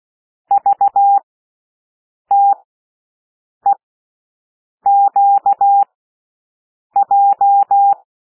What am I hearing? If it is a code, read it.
Morse code, heard as a keyed tone, about 16 wpm, spaced as for 6 wpm, 800 Hz. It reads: VTEQJ
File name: cwmp3 / L1 SOUND LETTERS 16 wpm